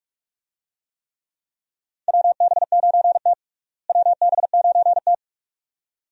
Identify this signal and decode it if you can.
Morse code, heard as a keyed tone, about 41 wpm, 700 Hz. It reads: WB9T WB9T
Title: USA Call Signs - 4 Characters - Repeated 2x - 45wpm